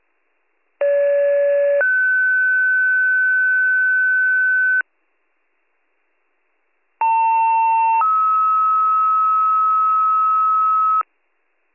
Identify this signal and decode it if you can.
Two-tone page, heard as a one-second tone, then a three-second tone, about 570 then 1500 Hz, about 900 then 1290 Hz. It is heard from 0.8 to 11.0 s.